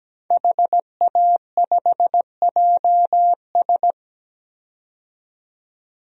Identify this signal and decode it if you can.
Morse code, heard as a keyed tone, about 17 wpm, 700 Hz. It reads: HA5JS
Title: DXCC-100 Call Signs - 5 Characters - 17wpm